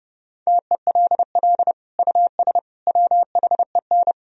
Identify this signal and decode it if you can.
Morse code, heard as a keyed tone, about 30 wpm, 700 Hz. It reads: TELL US WHEN
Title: Sentences from Top 1000 Words - Review Part 2 - 30wpm